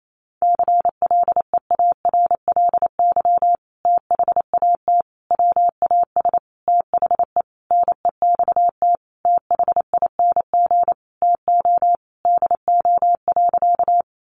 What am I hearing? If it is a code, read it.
Morse code, heard as a keyed tone, about 28 wpm, 700 Hz. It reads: CLEARLY THAT WAS THE NEXT THING TO DO.
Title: Sentences from Top 1300 Words 28wpm